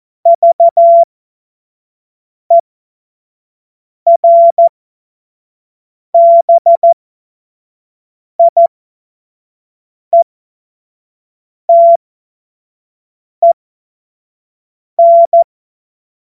Morse code, heard as a keyed tone, about 14 wpm, spaced as for 5 wpm, 680 Hz. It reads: VERBIETEN
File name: Morsekurs / Witze / Witz 28. März 2023 25 F70